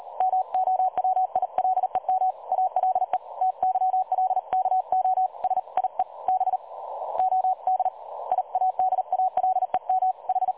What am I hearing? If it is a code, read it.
Morse code, heard as a keyed tone, about 38 wpm, spaced as for 31 wpm, 750 Hz. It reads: MOWILEM ALE TYLKO SIE BODIADALEM H
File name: dlazaawansowanych / nagraniazpasma / websdr recording 3565.0kHz